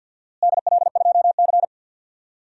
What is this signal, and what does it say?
Morse code, heard as a keyed tone, about 52 wpm, 700 Hz. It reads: DL2C
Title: DXCC-100 Call Signs - 4 Characters - Repeated 2x - 50wpm